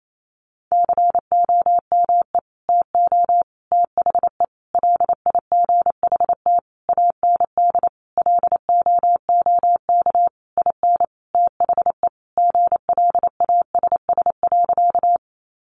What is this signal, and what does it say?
Morse code, heard as a keyed tone, about 28 wpm, 700 Hz. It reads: COME TO THE LIGHT AND LOOK IN THE GLASS.
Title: Sentences from Top 1300 Words 28wpm